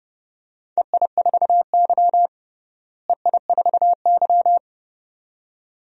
Morse code, heard as a keyed tone, about 29 wpm, 700 Hz. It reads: EI4Y EI4Y
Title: DXCC-100 Call Signs - 4 Characters - Repeated 2x - 30wpm